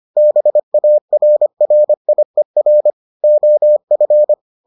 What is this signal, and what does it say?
Morse code, heard as a keyed tone, about 25 wpm, 600 Hz. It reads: BARRIER OF